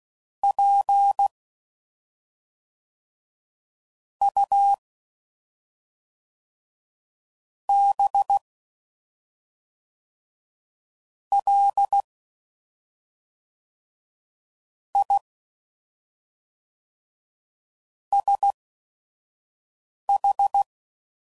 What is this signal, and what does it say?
Morse code, heard as a keyed tone, about 16 wpm, spaced as for 5 wpm, 800 Hz. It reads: P U B L I SH
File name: Suffixes / Special ING's / SlowToFaster ING's